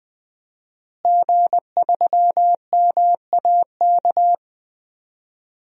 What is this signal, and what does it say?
Morse code, heard as a keyed tone, about 20 wpm, 700 Hz. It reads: G3MAK
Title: DXCC-100 Call Signs - 5 Characters - Repeated 2x - 20wpm